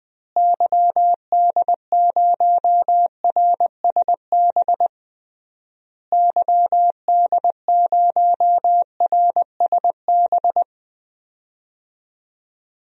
Morse code, heard as a keyed tone, about 20 wpm, 700 Hz. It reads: YD0RSB YD0RSB
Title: DXCC-100 Call Signs - 6 Characters - Repeated 3x - 20wpm